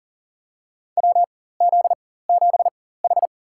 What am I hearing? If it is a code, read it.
Morse code, heard as a keyed tone, about 37 wpm, spaced as for 18 wpm, 700 Hz. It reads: WZ7H
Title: USA Call Signs - 4x Character Spacing - 40wpm